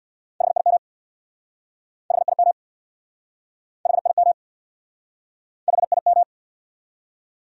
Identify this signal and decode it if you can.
Morse code, heard as a keyed tone, about 69 wpm, 700 Hz. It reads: HIM HIM HIM HIM